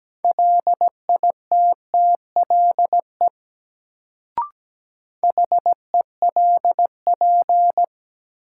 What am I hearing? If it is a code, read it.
Morse code, heard as a keyed tone, about 17 wpm, 700 Hz. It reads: LITTLE HELP
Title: Sentences from Top 200 Words - New Word Drill - 17wpm